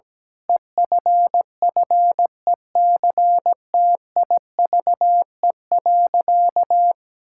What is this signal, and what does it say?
Morse code, heard as a keyed tone, about 17 wpm, 700 Hz. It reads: EFFECTIVE.
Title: Sentences from Top 1000 Words - Review Part 6 - 17wpm